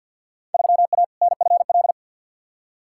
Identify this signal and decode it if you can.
Morse code, heard as a keyed tone, about 48 wpm, 700 Hz. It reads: 3A NFL
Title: Field Day Exchanges - 50wpm